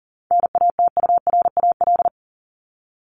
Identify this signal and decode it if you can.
Morse code, heard as a keyed tone, about 40 wpm, 700 Hz. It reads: NATURAL
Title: Top 501-750 Words 40wpm